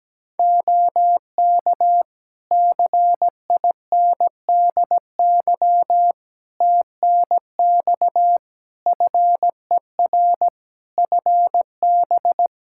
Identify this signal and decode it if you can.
Morse code, heard as a keyed tone, about 17 wpm, 700 Hz. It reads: OK CINDY TNX FER FB
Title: K1USN SST QSO Practice 17wpm